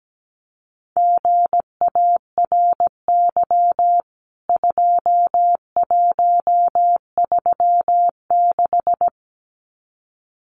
Morse code, heard as a keyed tone, about 17 wpm, 700 Hz. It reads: GARY 2136
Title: CWT Exchange 17wpm